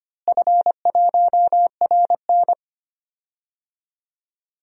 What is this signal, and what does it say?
Morse code, heard as a keyed tone, about 25 wpm, 700 Hz. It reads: F1RN
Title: DXCC-100 Call Signs - 4 Characters - Repeated 2x - 25wpm